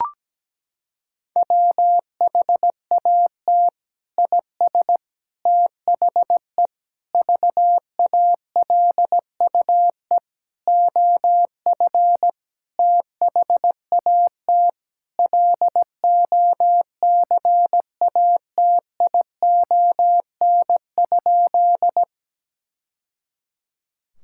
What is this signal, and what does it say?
Morse code, heard as a keyed tone, about 17 wpm, 700 Hz. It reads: WHAT IS THE VALUE OF THAT LOCATION?
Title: Sentences from Top 1000 Words - Review Part 6 - 17wpm